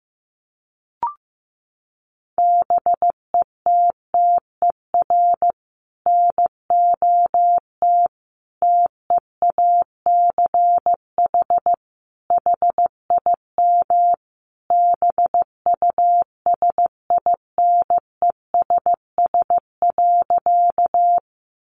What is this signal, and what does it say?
Morse code, heard as a keyed tone, about 15 wpm, 700 Hz. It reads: BETTER NOT TEACH HIM BUSINESS.
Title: Sentences from Top 600 Words 15wpm